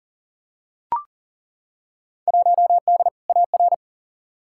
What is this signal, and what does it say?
Morse code, heard as a keyed tone, about 40 wpm, 700 Hz. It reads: 1D AR
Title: Field Day Exchanges - 40wpm